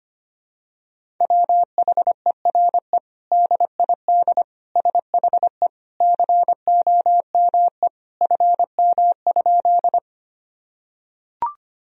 Morse code, heard as a keyed tone, about 25 wpm, 700 Hz. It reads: WHERE DID SHE COME FM?
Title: Sentences from Top 200 Words and Abbreviations 25wpm